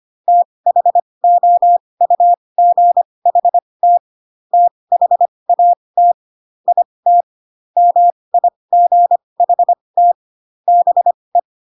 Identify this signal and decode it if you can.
Morse code, heard as a keyed tone, about 25 wpm, spaced as for 20 wpm, 700 Hz. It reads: THOUGHT THAT IT MIGHT BE